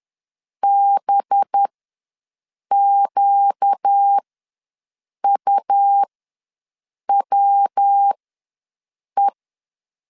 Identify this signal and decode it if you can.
Morse code, heard as a keyed tone, about 10 wpm, spaced as for 6 wpm, 800 Hz. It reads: BQUWE
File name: LEZIONI / 5 LBVC / 5 LB ws 10wpm 3616